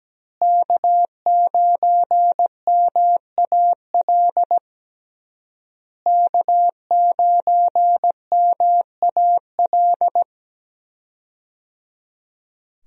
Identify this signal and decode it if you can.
Morse code, heard as a keyed tone, about 17 wpm, 700 Hz. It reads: K9MAL K9MAL
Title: USA Call Signs - 5 Characters - Repeated 3x - 17wpm